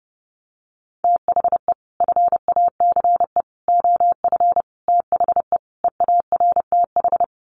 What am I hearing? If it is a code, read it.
Morse code, heard as a keyed tone, about 30 wpm, 700 Hz. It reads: THE FACE OF THE EARTH
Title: Sets of 5 Words Encore 30wpm